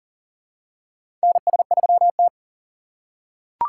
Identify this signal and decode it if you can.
Morse code, heard as a keyed tone, about 40 wpm, 700 Hz. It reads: NS3T
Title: USA Call Signs - 2x Character Spacing - 40wpm